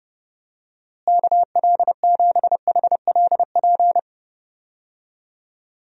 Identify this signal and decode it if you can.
Morse code, heard as a keyed tone, about 31 wpm, 700 Hz. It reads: KL7HLP
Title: DXCC-100 Call Signs - 6 Characters - Repeated 3x - 30wpm